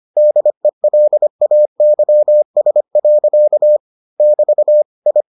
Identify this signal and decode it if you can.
Morse code, heard as a keyed tone, about 25 wpm, 600 Hz. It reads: DELAYS. <BT> I